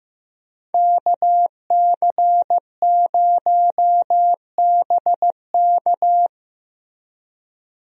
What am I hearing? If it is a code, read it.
Morse code, heard as a keyed tone, about 15 wpm, 700 Hz. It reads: KC0BK
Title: USA Call Signs - 5 Characters - Repeated 3x - 15wpm